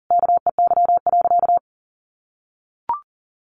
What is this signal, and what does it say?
Morse code, heard as a keyed tone, about 40 wpm, 700 Hz. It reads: KEY.
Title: Sentences from Top 600 Words 40wpm